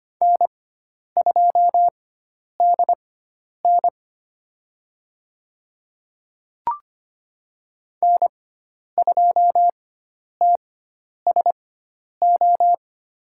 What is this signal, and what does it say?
Morse code, heard as a keyed tone, about 25 wpm, spaced as for 10 wpm, 700 Hz. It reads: N2DN N2TSO
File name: Lesson 16g - Letter c - Call Signs 25wpm